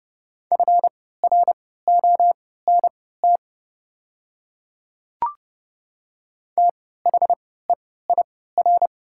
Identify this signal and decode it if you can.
Morse code, heard as a keyed tone, about 30 wpm, spaced as for 17 wpm, 700 Hz. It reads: FRONT THEIR
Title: 3x Character Spacing - Top 500 Words - 30wpm